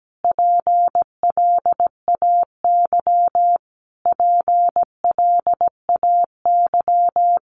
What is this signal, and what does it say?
Morse code, heard as a keyed tone, about 17 wpm, 700 Hz. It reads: PLAY PLAY
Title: Top 300 Words Repeated 17wpm